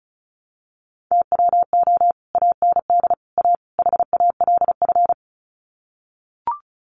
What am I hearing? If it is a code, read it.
Morse code, heard as a keyed tone, about 34 wpm, 700 Hz. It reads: TWO AND A HALF
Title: Sets of 4 Words 35wpm